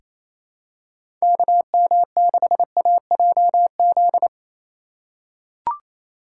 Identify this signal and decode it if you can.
Morse code, heard as a keyed tone, about 28 wpm, 700 Hz. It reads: KM6AJZ
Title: USA Call Signs - 6x Character Spacing - 28wpm